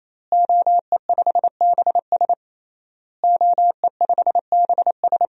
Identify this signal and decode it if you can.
Morse code, heard as a keyed tone, about 28 wpm, 700 Hz. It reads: OE5BS OE5BS
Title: DXCC-100 Call Signs - 5 Characters - Repeated 3x - 28wpm